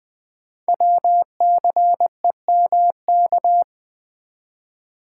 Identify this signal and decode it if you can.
Morse code, heard as a keyed tone, about 20 wpm, 700 Hz. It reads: WCEMK
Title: Code Groups - 5 Letters - 20wpm